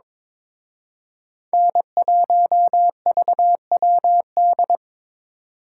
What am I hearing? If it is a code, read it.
Morse code, heard as a keyed tone, about 26 wpm, 700 Hz. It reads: N1VWD
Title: Speed Racing - USA Call Signs 22wpm